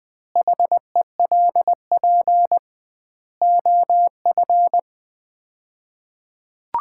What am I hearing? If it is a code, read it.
Morse code, heard as a keyed tone, about 20 wpm, 700 Hz. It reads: HELP OF
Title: 2x Word-Spacing - Sets of 4 Words - 20wpm